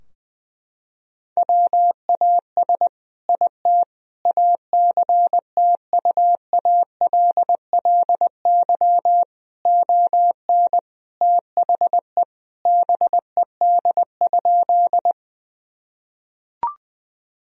Morse code, heard as a keyed tone, about 20 wpm, 700 Hz. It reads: WAS IT ACTUALLY ON THE BED?
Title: Sentences from Top 1000 Words - Review Part 4 - 20wpm